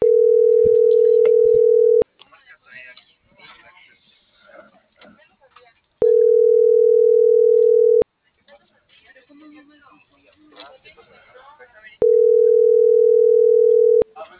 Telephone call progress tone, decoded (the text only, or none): ringback tone